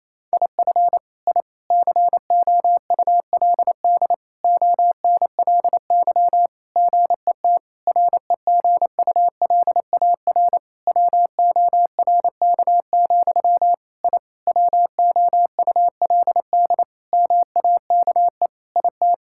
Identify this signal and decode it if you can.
Morse code, heard as a keyed tone, about 28 wpm, 700 Hz. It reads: IF I COULD ONLY GET REGULAR WORK, I WOULD MAKE IT
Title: Sentences from Top 1000 Words - Review Part 6 - 28wpm